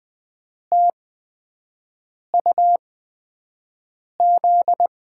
Morse code, heard as a keyed tone, about 20 wpm, spaced as for 5 wpm, 700 Hz. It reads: TUZ
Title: USA Call Signs - 5 Characters - 8x Character Spacing - 20wpm